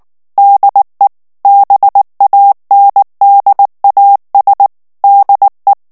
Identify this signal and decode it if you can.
Morse code, heard as a keyed tone, about 19 wpm, 800 Hz. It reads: DE BANDAS DE
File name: morse mp3 / ESPAGNOL TEXTE EN CLAIR 2 VITESSE 18 MOTS MINUTES